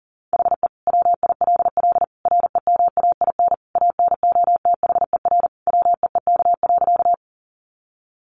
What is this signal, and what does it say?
Morse code, heard as a keyed tone, about 40 wpm, 700 Hz. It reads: HE WILL REMAIN ANOTHER WEEK.
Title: Sentences from Top 1200 Words 40wpm